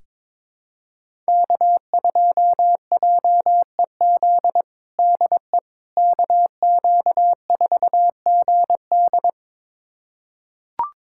Morse code, heard as a keyed tone, about 22 wpm, 700 Hz. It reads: K2JEZ DE KQ4GD
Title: USA Callsign DE Callsign - 5 Characters - 22wpm